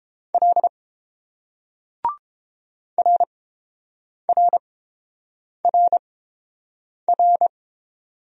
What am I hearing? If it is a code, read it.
Morse code, heard as a keyed tone, about 29 wpm, spaced as for 7 wpm, 700 Hz. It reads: L RRRR